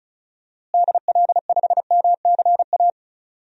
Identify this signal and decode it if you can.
Morse code, heard as a keyed tone, about 36 wpm, 700 Hz. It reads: DL5MCA
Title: DXCC-100 Call Signs - 6 Characters - Repeated 2x - 35wpm